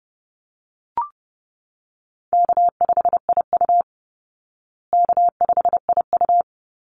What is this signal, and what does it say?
Morse code, heard as a keyed tone, about 30 wpm, 700 Hz. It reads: K5IU K5IU
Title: Call Signs Repeated 2x - 30wpm